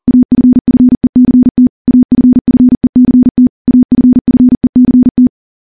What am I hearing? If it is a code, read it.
Morse code, heard as a keyed tone, about 39 wpm, 250 Hz. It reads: AFFECT AFFECT AFFECT
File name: morse / morse-training-40WPM